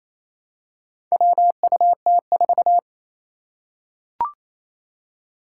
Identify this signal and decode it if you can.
Morse code, heard as a keyed tone, about 28 wpm, 700 Hz. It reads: WUT4
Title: Four Challenging Letter-Number Combinations 28wpm